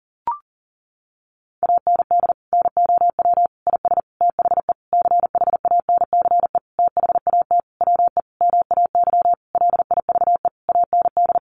Morse code, heard as a keyed tone, about 40 wpm, 700 Hz. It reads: AND NOW IS THE CHANCE THAT WE MAY LIVE AND
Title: Sentences from Top 800 Words 40wpm